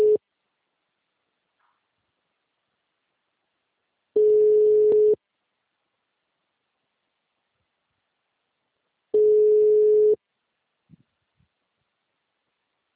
Ringback tone (Europe, 425 Hz).